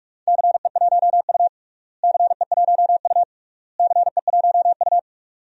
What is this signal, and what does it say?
Morse code, heard as a keyed tone, about 46 wpm, 700 Hz. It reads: CE1U CE1U CE1U
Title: DXCC-100 Call Signs - 4 Characters - Repeated 3x - 45wpm